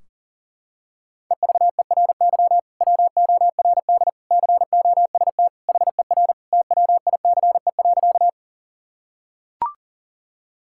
Morse code, heard as a keyed tone, about 39 wpm, 700 Hz. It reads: EVERY WORD COST HER TWICE.
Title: Sentences from Top 1600 Words 40wpm